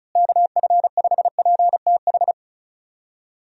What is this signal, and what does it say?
Morse code, heard as a keyed tone, about 36 wpm, 700 Hz. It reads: KF5PTH
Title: USA Call Signs - 6 Characters - Repeated 2x - 35wpm